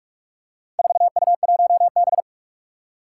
Morse code, heard as a keyed tone, about 46 wpm, 700 Hz. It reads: 4U1B
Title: Four Challenging Letter-Number Combinations 45wpm